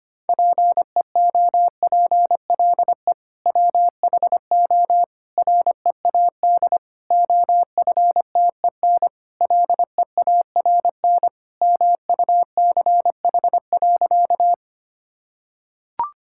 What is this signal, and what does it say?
Morse code, heard as a keyed tone, about 25 wpm, 700 Hz. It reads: PEOPLE WHO READ OFTEN LEARN MUCH.
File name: practice / 25wpm / Sentences with 2-Word Phrases - 25wpm